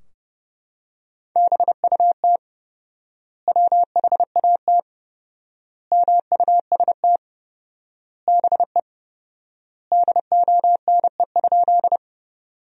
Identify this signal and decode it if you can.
Morse code, heard as a keyed tone, about 30 wpm, 700 Hz. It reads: BUT WHAT MUST BE DONE?
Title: 4x Word-Spacing - Sentences from Top 500 Words - 30wpm